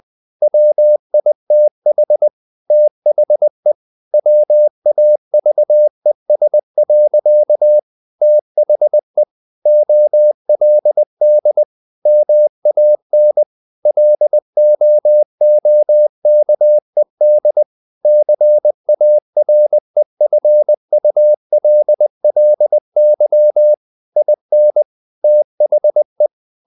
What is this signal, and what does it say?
Morse code, heard as a keyed tone, about 20 wpm, 600 Hz. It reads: WITH THE WAVES. THE OLD MAN LOOKED CAREFULLY IN THE
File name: the old man and the sea part 2 20wpm